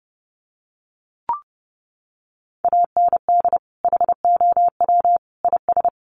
Morse code, heard as a keyed tone, about 30 wpm, 700 Hz. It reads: AND HOW IS
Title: Sentences from Top 1400 Words 30wpm